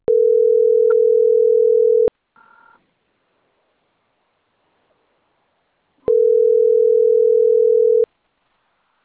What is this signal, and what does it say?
Ringback tone (North America), 440 + 480 Hz.